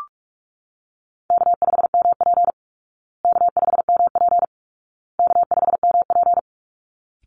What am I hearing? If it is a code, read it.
Morse code, heard as a keyed tone, about 44 wpm, 700 Hz. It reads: K5MP K5MP K5MP
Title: Call Signs Repeated 2x - 45wpm